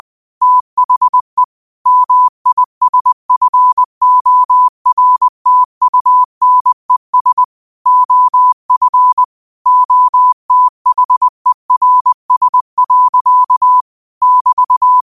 Morse code, heard as a keyed tone, about 20 wpm, 1000 Hz. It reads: THE MISFORTUNES OF OTHERS. <BT>
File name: qotdfiles / 20wpm